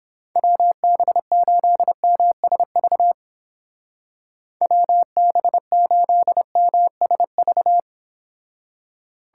Morse code, heard as a keyed tone, about 31 wpm, 700 Hz. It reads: WB8MSV WB8MSV